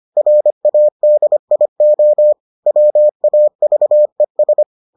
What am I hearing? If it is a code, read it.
Morse code, heard as a keyed tone, about 25 wpm, 600 Hz. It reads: RADIO WAVES